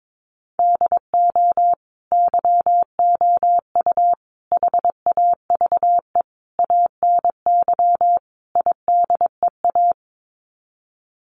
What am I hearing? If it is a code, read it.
Morse code, heard as a keyed tone, about 22 wpm, 700 Hz. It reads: DO YOU HAVE ANY IDEA
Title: Sets of 5 Words Encore 22wpm